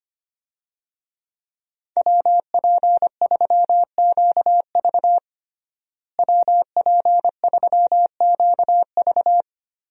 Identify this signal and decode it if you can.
Morse code, heard as a keyed tone, about 25 wpm, 700 Hz. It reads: WP3QV WP3QV
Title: USA Call Signs - 5 Characters - Repeated 3x - 25wpm